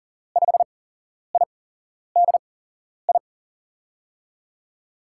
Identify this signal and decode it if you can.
Morse code, heard as a keyed tone, about 40 wpm, spaced as for 11 wpm, 700 Hz. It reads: 5IDI
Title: USA Call Signs - 6 Characters - 8x Character Spacing - 40wpm